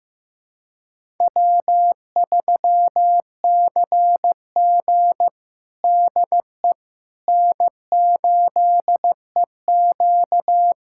Morse code, heard as a keyed tone, about 15 wpm, 700 Hz. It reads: W3CG DE N8EQ
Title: USA Callsign DE Callsign - 4 Characters - 15wpm